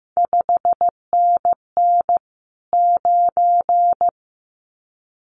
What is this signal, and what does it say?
Morse code, heard as a keyed tone, about 15 wpm, 700 Hz. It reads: 5NN 9
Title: CQ WW DX Contest Exchange - 15wpm